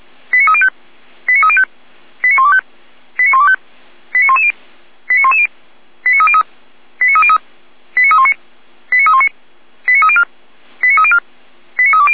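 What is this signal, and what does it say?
Five-tone selective call (ZVEI), heard as quick runs of five tones, tones of about 70 ms. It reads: 89305 89305 89125 89125 89110 89110 89303 89303 89319 89319 89304 89304 89320
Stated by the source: Five-tone ZVEI selective call received by SDR.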